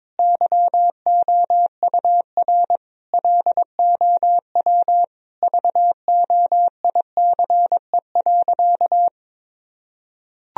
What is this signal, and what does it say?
Morse code, heard as a keyed tone, about 22 wpm, 700 Hz. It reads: YOUR LOW VOICE.
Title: Sentences from Top 1000 Words - Review Part 1 - 22wpm